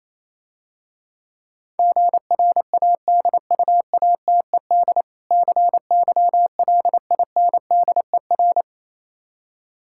Morse code, heard as a keyed tone, about 27 wpm, 700 Hz. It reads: GRADUATED CYLINDER
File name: Easy Words - Speed-Racing with Hints - 22wpm